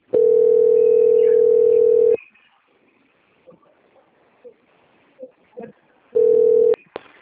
Ringback tone (North America), 440 + 480 Hz.
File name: out- - - - - .29703